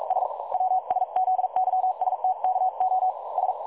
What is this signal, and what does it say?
Morse code, heard as a keyed tone, about 42 wpm, 750 Hz. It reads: S WSZYSTKO S
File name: dlazaawansowanych / nagraniazpasma / websdr recording 3565.0kHz